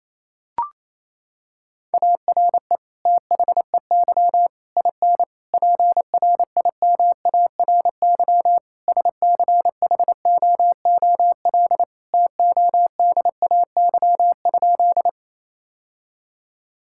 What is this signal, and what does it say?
Morse code, heard as a keyed tone, about 28 wpm, 700 Hz. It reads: ARE THEY IN PRIMARY SCHOOL TODAY?
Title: Sentences from Top 1000 Words - Review Part 5 - 28wpm